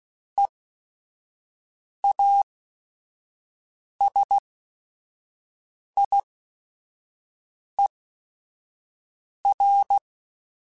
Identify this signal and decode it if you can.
Morse code, heard as a keyed tone, about 16 wpm, spaced as for 5 wpm, 800 Hz. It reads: EASIER